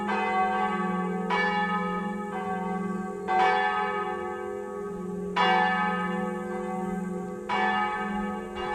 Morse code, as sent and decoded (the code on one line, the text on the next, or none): none